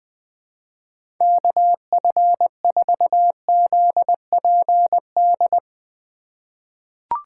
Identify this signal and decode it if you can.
Morse code, heard as a keyed tone, about 21 wpm, 700 Hz. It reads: KF4ZPD
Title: USA Call Signs - 6 Characters - 4x Character Spacing - 20wpm